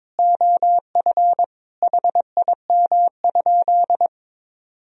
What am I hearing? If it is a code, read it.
Morse code, heard as a keyed tone, about 22 wpm, 700 Hz. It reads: OF HIM?
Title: Sentences from Top 200 Words and Abbreviations 22wpm